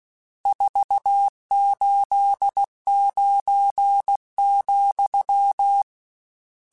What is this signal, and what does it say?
Morse code, heard as a keyed tone, about 16 wpm, 800 Hz. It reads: 489,
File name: RSTs / RST-mp3s / Various RST @15